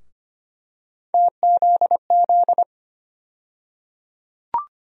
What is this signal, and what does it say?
Morse code, heard as a keyed tone, about 25 wpm, 700 Hz. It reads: TZZ
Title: USA Call Sign Postfixes - 3 Letters 25wpm